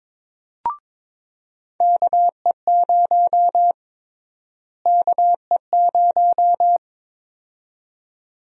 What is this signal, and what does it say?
Morse code, heard as a keyed tone, about 22 wpm, 700 Hz. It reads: KE0 KE0
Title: USA Call Sign Prefixes - Repeated 2x - 22wpm